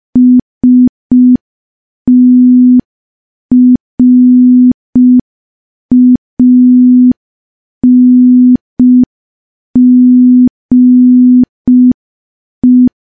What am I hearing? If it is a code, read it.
Morse code, heard as a keyed tone, about 5 wpm, 250 Hz. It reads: STRANGE
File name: morse / morse-training-05WPM